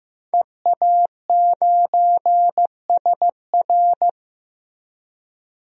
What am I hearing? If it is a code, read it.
Morse code, heard as a keyed tone, about 16 wpm, 700 Hz. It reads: EA9SR
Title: DXCC-100 Call Signs - 5 Characters - Repeated 2x - 15wpm